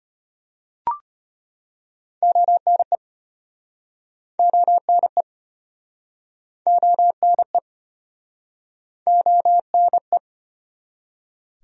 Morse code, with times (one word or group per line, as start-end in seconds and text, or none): 2.2-3.0 ONE
4.4-5.2 ONE
6.7-7.6 ONE
9.1-10.2 ONE